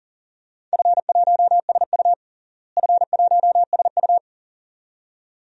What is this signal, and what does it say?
Morse code, heard as a keyed tone, about 40 wpm, 700 Hz. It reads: F1SU F1SU
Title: DXCC-100 Call Signs - 4 Characters - Repeated 2x - 40wpm